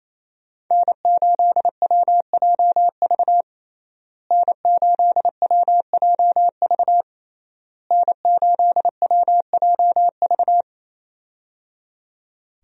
Morse code, heard as a keyed tone, about 28 wpm, 700 Hz. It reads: N8WJV N8WJV N8WJV
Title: USA Call Signs - 5 Characters - Repeated 3x - 28wpm